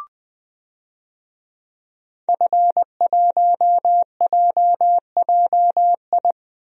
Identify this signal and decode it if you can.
Morse code, heard as a keyed tone, about 20 wpm, 700 Hz. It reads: F1JJI
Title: DXCC-100 Call Signs - 5 Characters - Repeated 3x - 20wpm